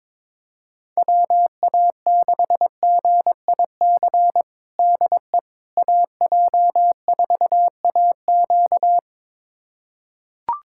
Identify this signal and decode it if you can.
Morse code, heard as a keyed tone, about 22 wpm, 700 Hz. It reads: WA6GIC DE AJ4AQ
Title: USA Callsign DE Callsign 22wpm